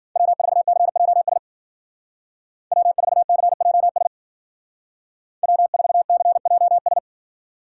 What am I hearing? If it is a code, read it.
Morse code, heard as a keyed tone, about 50 wpm, 700 Hz. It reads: W4CJS W4CJS W4CJS
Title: Speed Racing - USA 1x3 Call Signs 40wpm